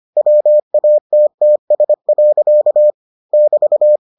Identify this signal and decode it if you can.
Morse code, heard as a keyed tone, about 25 wpm, 600 Hz. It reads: WATTS. <BT>